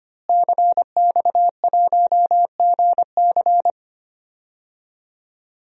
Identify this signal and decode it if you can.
Morse code, heard as a keyed tone, about 25 wpm, 700 Hz. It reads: CX1GC
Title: DXCC-100 Call Signs - 5 Characters - Repeated 3x - 25wpm